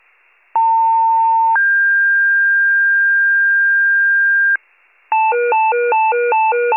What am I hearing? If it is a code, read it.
Two-tone page, heard as a one-second tone, then a three-second tone, about 900 then 1600 Hz. It is heard from 0.5 to 4.6 s.